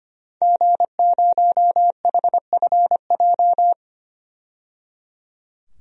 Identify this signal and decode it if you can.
Morse code, heard as a keyed tone, about 25 wpm, 700 Hz. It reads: G0HFJ
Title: DXCC-100 Call Signs - 5 Characters - 25wpm